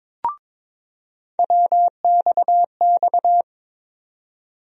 Morse code, heard as a keyed tone, about 22 wpm, 700 Hz. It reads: WXX
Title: USA Call Sign Postfixes - 3 Letters 22wpm